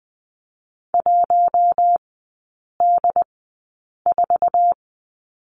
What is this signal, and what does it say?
Morse code, heard as a keyed tone, about 20 wpm, spaced as for 8 wpm, 700 Hz. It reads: 1D4
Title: 067 Lesson 12e - Letter d - Three Characters 20wpm